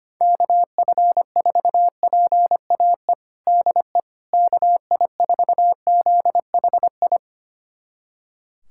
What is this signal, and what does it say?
Morse code, heard as a keyed tone, about 25 wpm, 700 Hz. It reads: KF4PAE DE KI4ZHI
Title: USA Callsign DE Callsign - 6 Characters - 25wpm